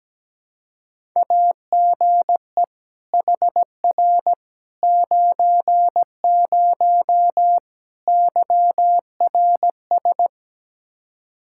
Morse code, heard as a keyed tone, about 17 wpm, 700 Hz. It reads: AGE HR 90 YRS
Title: Practice QSO - Exchange 3 - More Info - 17wpm